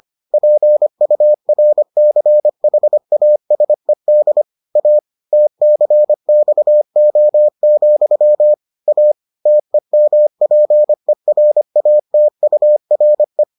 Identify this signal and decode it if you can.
Morse code, heard as a keyed tone, about 25 wpm, 600 Hz. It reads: PURCHASED A TCXO, A TEMPERATURE